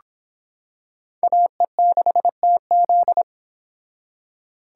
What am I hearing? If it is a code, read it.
Morse code, heard as a keyed tone, about 26 wpm, 700 Hz. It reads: AE6TZ